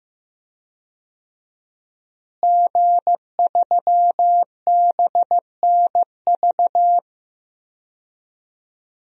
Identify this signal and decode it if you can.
Morse code, heard as a keyed tone, about 15 wpm, 700 Hz. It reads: G3BNV
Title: DXCC-100 Call Signs - 5 Characters - Repeated 3x - 15wpm